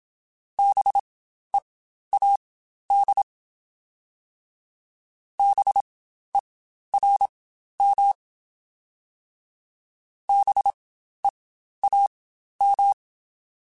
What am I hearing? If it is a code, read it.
Morse code, heard as a keyed tone, about 27 wpm, spaced as for 12 wpm, 800 Hz. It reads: BEAD BERM BEAM
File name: Spaced Alphas / Spaced Alphas mp3s / Bs sp @25